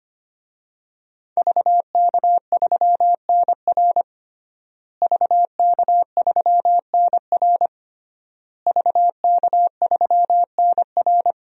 Morse code, heard as a keyed tone, about 25 wpm, 700 Hz. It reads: VK3NR VK3NR VK3NR
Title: DXCC-100 Call Signs - 5 Characters - Repeated 3x - 25wpm